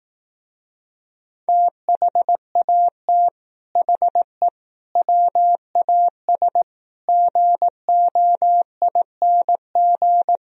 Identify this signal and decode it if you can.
Morse code, heard as a keyed tone, about 18 wpm, 700 Hz. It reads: THAT HE WAS GOING
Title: Speed-Racing - Sets of 5 Words 15wpm